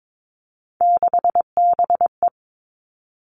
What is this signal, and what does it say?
Morse code, heard as a keyed tone, about 22 wpm, 700 Hz. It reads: 6BE
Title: Three Challenging Letter-Number Combinations 22wpm